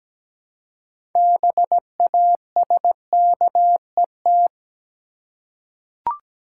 Morse code, heard as a keyed tone, about 17 wpm, 700 Hz. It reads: BASKET
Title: Six Letter Words - 17wpm